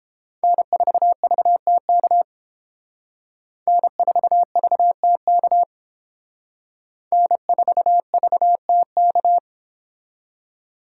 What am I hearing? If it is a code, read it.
Morse code, heard as a keyed tone, about 31 wpm, 700 Hz. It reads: N4VTK N4VTK N4VTK